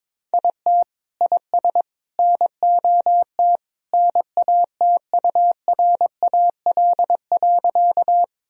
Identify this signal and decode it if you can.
Morse code, heard as a keyed tone, about 22 wpm, 700 Hz. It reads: IT IS NOT NATURAL.
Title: Sentences from Top 1000 Words - Review Part 4 - 22wpm